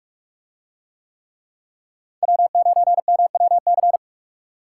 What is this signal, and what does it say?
Morse code, heard as a keyed tone, about 41 wpm, 700 Hz. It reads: W9MWC